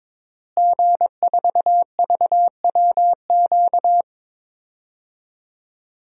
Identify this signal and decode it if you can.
Morse code, heard as a keyed tone, about 22 wpm, 700 Hz. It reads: G4VWQ
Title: DXCC-100 Call Signs - 5 Characters - Repeated 2x - 22wpm